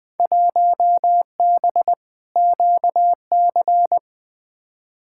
Morse code, heard as a keyed tone, about 20 wpm, 700 Hz. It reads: 1B QC
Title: Field Day Exchanges Encore - 20wpm